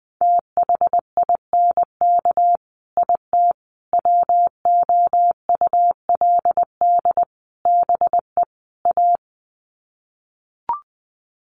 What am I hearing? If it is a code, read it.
Morse code, heard as a keyed tone, about 20 wpm, 700 Hz. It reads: THINK IT WOULD BE A
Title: Sets of 5 Words Encore2 20wpm